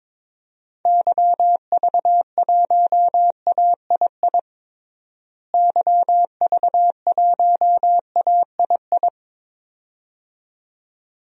Morse code, heard as a keyed tone, about 22 wpm, 700 Hz. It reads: YV1AII YV1AII
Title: DXCC-100 Call Signs - 6 Characters - Repeated 2x - 22wpm